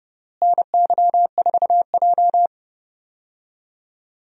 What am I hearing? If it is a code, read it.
Morse code, heard as a keyed tone, about 30 wpm, 700 Hz. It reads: NY4J
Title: USA Call Signs - 4 Characters - Repeated 3x - 30wpm